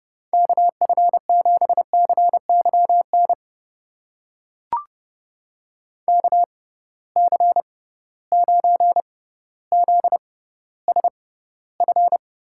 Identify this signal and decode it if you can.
Morse code, heard as a keyed tone, about 30 wpm, spaced as for 15 wpm, 700 Hz. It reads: KF7CYN KC9ZSF
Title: USA Call Signs - 6 Characters - 6x Character Spacing - 30wpm